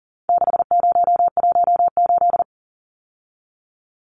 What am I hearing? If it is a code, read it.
Morse code, heard as a keyed tone, about 40 wpm, 700 Hz. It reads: 6018
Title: Numbers - 4-Digit 40wpm